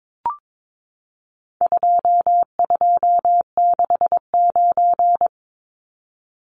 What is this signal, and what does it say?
Morse code, heard as a keyed tone, about 22 wpm, 700 Hz. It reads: 2269
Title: Numbers - 4-Digit 22wpm